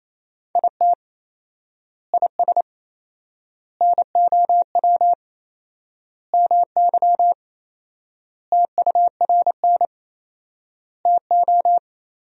Morse code, heard as a keyed tone, about 27 wpm, 700 Hz. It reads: IT IS NOW MY TURN TO
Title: 4x Word-Spacing - Sentences from Top 200 Words - 28wpm